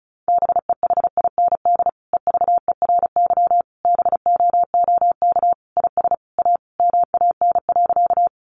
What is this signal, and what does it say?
Morse code, heard as a keyed tone, about 34 wpm, 700 Hz. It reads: BEHIND EVERY BOOK IS A MAN.
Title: Sentences from Top 500 Words 35wpm